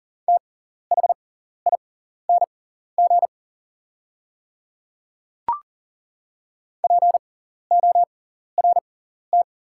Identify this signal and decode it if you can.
Morse code, heard as a keyed tone, about 40 wpm, spaced as for 14 wpm, 690 Hz. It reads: THING PORT
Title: 6x Character Spacing - Top 200 Words - 40wpm